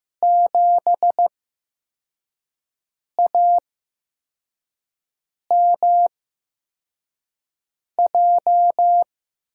Morse code, heard as a keyed tone, about 15 wpm, spaced as for 4 wpm, 700 Hz. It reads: 7AMJ
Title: USA Call Signs - 6 Characters - 8x Character Spacing - 15wpm